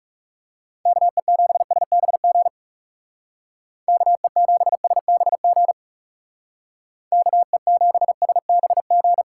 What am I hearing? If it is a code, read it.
Morse code, heard as a keyed tone, about 36 wpm, 700 Hz. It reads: KE7SBG KE7SBG KE7SBG
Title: Speed Racing - USA 2x3 Call Signs 30wpm